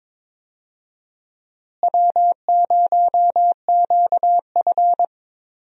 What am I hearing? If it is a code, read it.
Morse code, heard as a keyed tone, about 22 wpm, 700 Hz. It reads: W0QF
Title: USA Call Signs - 4 Characters - Repeated 2x - 22wpm